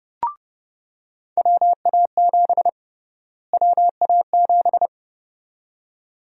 Morse code, heard as a keyed tone, about 32 wpm, 700 Hz. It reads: WA7 WA7
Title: USA Call Sign Prefixes - Repeated 2x - 30wpm